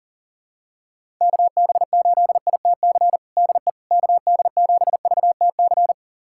Morse code, heard as a keyed tone, about 37 wpm, 700 Hz. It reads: KB8ITC DE KD7VTC
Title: USA Callsign DE Callsign - 6 Characters - 40wpm